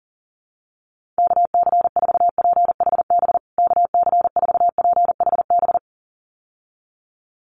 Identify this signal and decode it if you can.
Morse code, heard as a keyed tone, about 40 wpm, 700 Hz. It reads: KC4PHB KC4PHB
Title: USA Call Signs Repeated 2x 40wpm